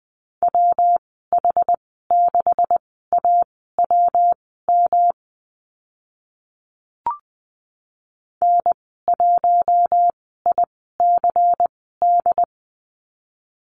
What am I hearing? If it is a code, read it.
Morse code, heard as a keyed tone, about 20 wpm, spaced as for 14 wpm, 700 Hz. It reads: WH6AWM N1ICD
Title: 128 Lesson 21g - Number 6 - Call Signs 20wpm